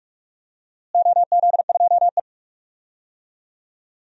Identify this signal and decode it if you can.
Morse code, heard as a keyed tone, about 46 wpm, 700 Hz. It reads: OZ2E
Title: DXCC-100 Call Signs - 4 Characters - Repeated 2x - 45wpm